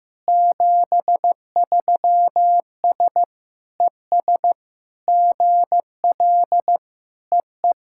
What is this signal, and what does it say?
Morse code, heard as a keyed tone, about 16 wpm, 700 Hz. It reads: 73S ES GL EE
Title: SOTA QSO Practice 15wpm